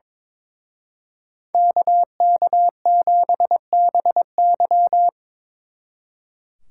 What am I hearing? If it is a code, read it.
Morse code, heard as a keyed tone, about 25 wpm, 700 Hz. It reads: KK7BY
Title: Speed Racing - USA 2x2 Call Signs 22wpm